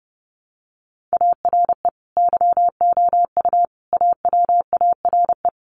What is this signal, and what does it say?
Morse code, heard as a keyed tone, about 30 wpm, 700 Hz. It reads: ARE YOU AWARE
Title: Sentences from Top 1500 Words 30wpm